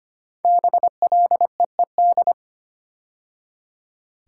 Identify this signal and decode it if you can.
Morse code, heard as a keyed tone, about 25 wpm, 700 Hz. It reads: BLEED
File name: 5 Letter Words - First 3 Letters Given - 25wpm